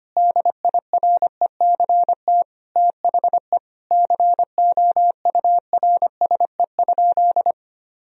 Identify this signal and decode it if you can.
Morse code, heard as a keyed tone, about 25 wpm, 700 Hz. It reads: DIRECT THE COURSE?
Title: Sentences from Top 1000 Words - Review Part 2 - 25wpm